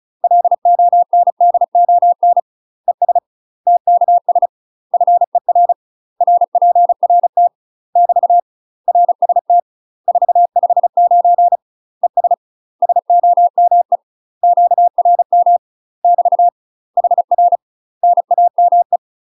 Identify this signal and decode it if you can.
Morse code, heard as a keyed tone, about 36 wpm, 700 Hz. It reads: LONDON ES TKS FER RPRT <BT> RST 459 ES SOME QRM <BT> HR NAME
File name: UK MASTER QSOs TRAINING / 35WPM / UK MASTER QSOs 0027